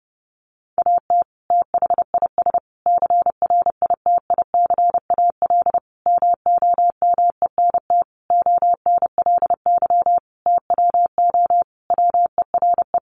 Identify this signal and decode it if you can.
Morse code, heard as a keyed tone, about 30 wpm, 700 Hz. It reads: AT THIS CRITICAL MOMENT ONLY TWO WERE
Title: Sentences from Top 1100 Words 30wpm